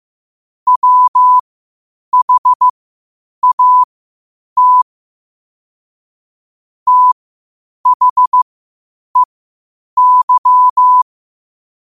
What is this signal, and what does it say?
Morse code, heard as a keyed tone, about 15 wpm, spaced as for 8 wpm, 1000 Hz. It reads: WHAT THEY
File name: qotdfiles / 8wpm